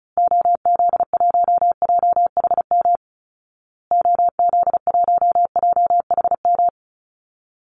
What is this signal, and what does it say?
Morse code, heard as a keyed tone, about 34 wpm, 700 Hz. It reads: OZ1JHM OZ1JHM
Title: Call Signs Repeated 2x - 35wpm